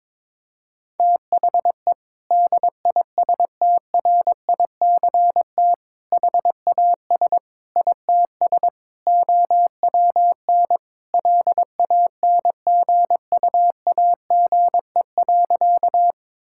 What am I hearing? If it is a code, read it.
Morse code, heard as a keyed tone, about 22 wpm, 700 Hz. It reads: THE DISTRICT HAS ITS OWN LANGUAGE.
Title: Sentences from Top 1600 Words 22wpm